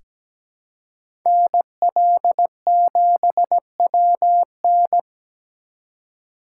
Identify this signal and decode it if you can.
Morse code, heard as a keyed tone, about 17 wpm, 700 Hz. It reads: NL7WN
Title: USA Call Signs - 5 Characters - 3x Character Spacing - 17wpm